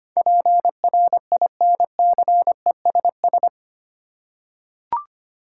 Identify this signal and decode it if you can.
Morse code, heard as a keyed tone, about 25 wpm, 700 Hz. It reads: PRINCESS
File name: Eight Letter Words - 25wpm